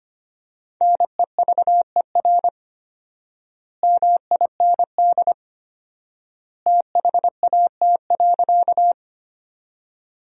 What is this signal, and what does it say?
Morse code, heard as a keyed tone, about 25 wpm, 700 Hz. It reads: NEVER MIND THAT.
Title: 4x Word-Spacing - Sentences from Top 500 Words - 25wpm